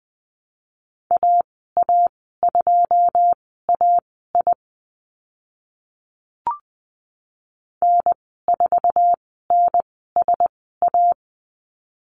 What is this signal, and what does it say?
Morse code, heard as a keyed tone, about 20 wpm, spaced as for 15 wpm, 700 Hz. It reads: AA2AI N4NSA
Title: 082 Lesson 14g - Number 2 - Call Signs 20wpm